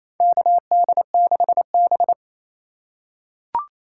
Morse code, heard as a keyed tone, about 28 wpm, 700 Hz. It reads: KD6B
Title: USA Call Signs - 4 Characters - 4x Character Spacing - 28wpm